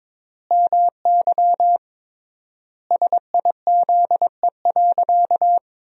Morse code, heard as a keyed tone, about 22 wpm, 700 Hz. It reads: MY SIZE.
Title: 3x Word-Spacing - Sentences from Top 400 Words - 22wpm